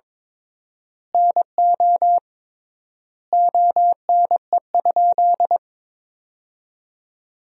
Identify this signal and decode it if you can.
Morse code, heard as a keyed tone, about 22 wpm, 700 Hz. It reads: NO ONE?
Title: 3x Word-Spacing - Sentences from Top 300 Words - 22wpm